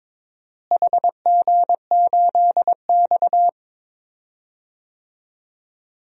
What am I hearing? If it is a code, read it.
Morse code, heard as a keyed tone, about 22 wpm, 700 Hz. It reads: HG8X
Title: DXCC-100 Call Signs - 4 Characters - Repeated 3x - 22wpm